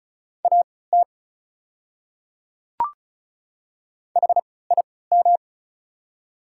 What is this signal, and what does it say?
Morse code, heard as a keyed tone, about 34 wpm, spaced as for 20 wpm, 690 Hz. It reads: AT HIM
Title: 3x Character Spacing - Top 100 Words - 35wpm